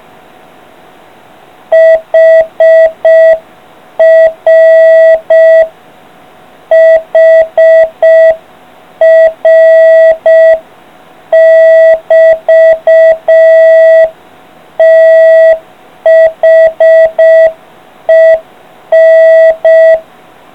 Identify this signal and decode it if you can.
Morse code, heard as a keyed tone, about 6 wpm, 660 Hz. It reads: HR HR <BT> THEN